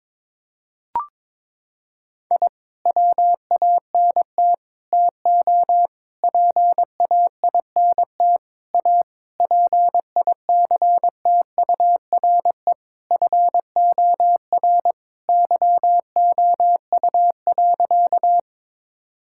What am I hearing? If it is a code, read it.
Morse code, heard as a keyed tone, about 22 wpm, 700 Hz. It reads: I WANT TO PAINT A PICTURE FOR YOU.
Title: Sentences from Top 1000 Words - Review Part 6 - 22wpm